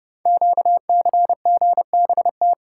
Morse code, heard as a keyed tone, about 30 wpm, 700 Hz. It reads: QCGBT
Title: Code Groups - 5 Letter:Numbers - 30wpm